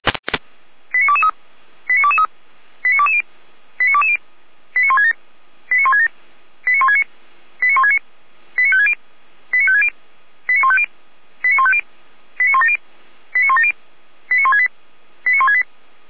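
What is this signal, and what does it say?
Five-tone selective call (ZVEI), heard as quick runs of five tones, tones of about 70 ms. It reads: 89223 89223 89220 89220 89167 89167 89169 89169 89560 89560 89150 89150 89170 89170 89168 89168
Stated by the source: ZVEI five-tone selective call 89163 received on SDR.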